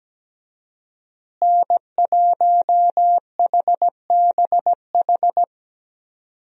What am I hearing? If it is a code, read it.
Morse code, heard as a keyed tone, about 17 wpm, 700 Hz. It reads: N1HBH
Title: USA Call Signs - 5 Characters - Repeated 2x - 17wpm